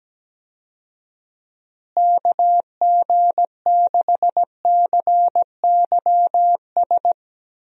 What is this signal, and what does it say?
Morse code, heard as a keyed tone, about 17 wpm, 700 Hz. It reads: KG6CYS
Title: USA Call Signs - 6 Characters - Repeated 2x - 17wpm